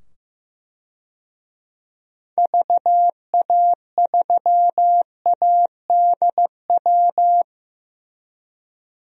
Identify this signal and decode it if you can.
Morse code, heard as a keyed tone, about 15 wpm, 700 Hz. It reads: VA3ADW
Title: DXCC-100 Call Signs - 6 Characters - 15wpm